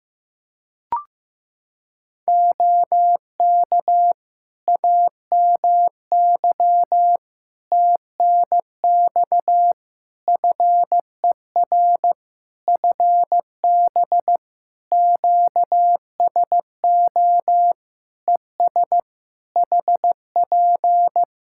Morse code, heard as a keyed tone, about 15 wpm, 700 Hz. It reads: OK AMY TNX FER FB QSO ES HP
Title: Practice QSO - Exchange 4 - Ending - 15wpm